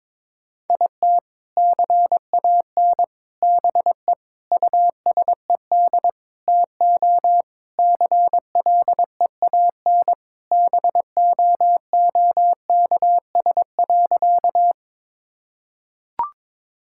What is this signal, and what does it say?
Morse code, heard as a keyed tone, about 22 wpm, 700 Hz. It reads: IT CAN BE USED TO CLEAN BOOKS.
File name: practice / 22wpm / Sentences with 4-Word Phrases - 22wpm